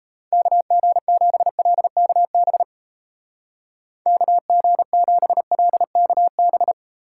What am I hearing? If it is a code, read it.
Morse code, heard as a keyed tone, about 31 wpm, 700 Hz. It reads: KG7LKB KG7LKB
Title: Speed Racing - USA 2x3 Call Signs 28wpm